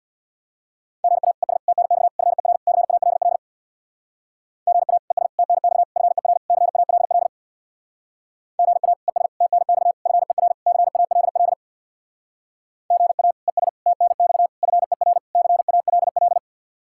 Morse code, heard as a keyed tone, about 64 wpm, 700 Hz. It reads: GA ES TNX FER CALL GA ES TNX FER CALL GA ES TNX FER CALL GA ES TNX FER CALL